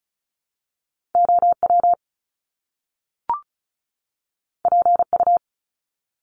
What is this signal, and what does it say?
Morse code, heard as a keyed tone, about 36 wpm, 700 Hz. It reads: OW PU
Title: Top Two Letter Combinations in Words 35wpm